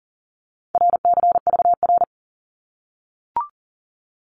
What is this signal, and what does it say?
Morse code, heard as a keyed tone, about 40 wpm, 700 Hz. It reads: RCVR
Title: Abbreviations 40wpm